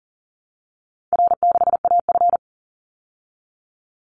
Morse code, heard as a keyed tone, about 40 wpm, 700 Hz. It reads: R6AF
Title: Easy Call Signs Repeated 2x - 40wpm